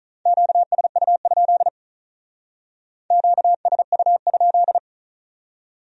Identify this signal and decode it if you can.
Morse code, heard as a keyed tone, about 44 wpm, 700 Hz. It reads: QSU? QSU?